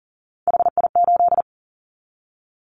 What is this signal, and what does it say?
Morse code, heard as a keyed tone, about 40 wpm, 700 Hz. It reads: HI8
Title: Callsign Prefixes for Top 100 DXCC - 40wpm